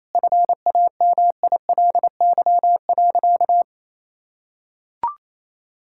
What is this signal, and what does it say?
Morse code, heard as a keyed tone, about 28 wpm, 700 Hz. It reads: FAMILY.
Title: Sentences from Top 1000 Words - Review Part 6 - 28wpm